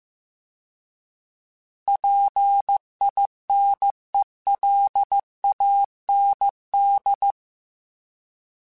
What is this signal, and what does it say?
Morse code, heard as a keyed tone, about 15 wpm, 800 Hz. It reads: PINELAND